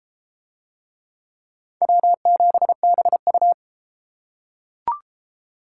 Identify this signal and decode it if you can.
Morse code, heard as a keyed tone, about 32 wpm, 700 Hz. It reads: W7BU